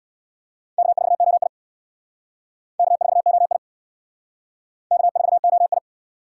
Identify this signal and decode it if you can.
Morse code, heard as a keyed tone, about 65 wpm, 690 Hz. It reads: K4OI K4OI K4OI